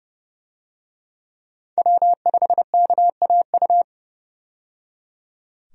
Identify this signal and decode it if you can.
Morse code, heard as a keyed tone, about 30 wpm, 700 Hz. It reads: W5KAU
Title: USA Call Signs - 5 Characters - 30wpm